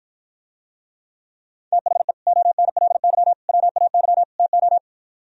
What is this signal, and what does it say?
Morse code, heard as a keyed tone, about 52 wpm, 700 Hz. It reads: THE ONLY WAY TO